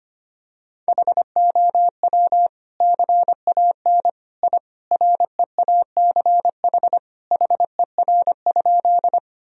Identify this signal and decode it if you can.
Morse code, heard as a keyed tone, about 25 wpm, 700 Hz. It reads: HOW CAN I REACH HER?
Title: Sentences from Top 1000 Words - Review Part 2 - 25wpm